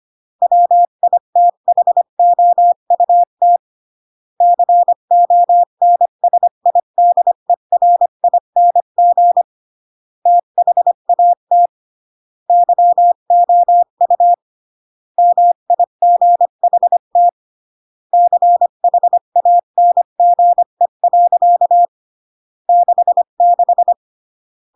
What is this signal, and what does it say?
Morse code, heard as a keyed tone, about 25 wpm, 700 Hz. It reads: WITHOUT CONSIDERING THAT YOU MIGHT CHANGE. 66